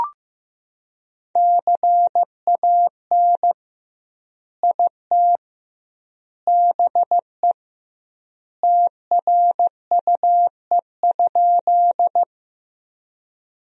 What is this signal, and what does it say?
Morse code, heard as a keyed tone, about 15 wpm, 700 Hz. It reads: CAN IT BE TRUE?
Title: 2x Word-Spacing - Sentences from Top 400 Words - 15wpm